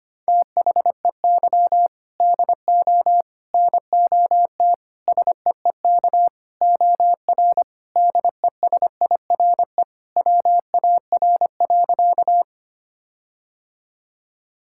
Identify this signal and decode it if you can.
Morse code, heard as a keyed tone, about 25 wpm, 700 Hz. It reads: THEY DO NOT SEEK OR DESIRE WAR.
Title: Sentences from Top 1600 Words 25wpm